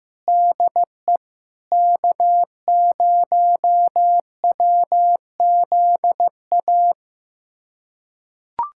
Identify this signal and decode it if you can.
Morse code, heard as a keyed tone, about 15 wpm, 700 Hz. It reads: DE K0WZA
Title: USA Callsign DE Callsign 15wpm